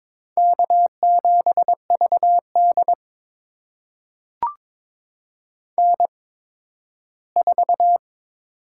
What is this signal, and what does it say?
Morse code, heard as a keyed tone, about 22 wpm, spaced as for 13 wpm, 700 Hz. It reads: K7VD N4
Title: USA Call Signs - 8x Character Spacing - 22wpm